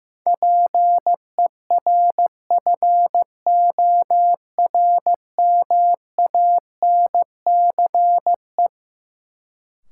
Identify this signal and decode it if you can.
Morse code, heard as a keyed tone, about 15 wpm, 700 Hz. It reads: PERFORMANCE
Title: Sentences from Top 1000 Words - New Word Drill - 15wpm